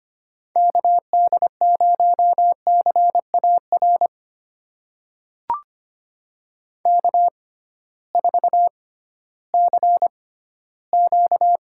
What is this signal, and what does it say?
Morse code, heard as a keyed tone, about 25 wpm, spaced as for 15 wpm, 700 Hz. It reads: KD0CAR K4CQ
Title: USA Call Signs - 6x Character Spacing - 25wpm